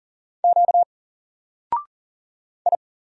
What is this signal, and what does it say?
Morse code, heard as a keyed tone, about 40 wpm, 700 Hz. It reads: Q I
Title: Single Letter-Number Alt 40wpm